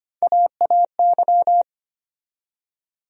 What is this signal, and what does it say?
Morse code, heard as a keyed tone, about 25 wpm, 700 Hz. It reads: AAY
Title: USA Call Sign Postfixes - 3 Letters 25wpm